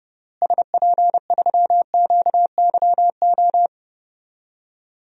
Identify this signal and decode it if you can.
Morse code, heard as a keyed tone, about 29 wpm, 700 Hz. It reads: SP3QYO
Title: DXCC-100 Call Signs - 6 Characters - Repeated 3x - 30wpm